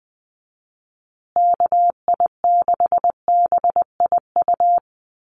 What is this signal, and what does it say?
Morse code, heard as a keyed tone, about 20 wpm, 700 Hz. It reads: KI6BIU
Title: USA Call Signs Repeated 3x 20wpm